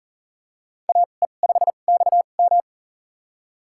Morse code, heard as a keyed tone, about 40 wpm, spaced as for 29 wpm, 700 Hz. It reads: AE5XM
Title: USA Call Signs - 2x Character Spacing - 40wpm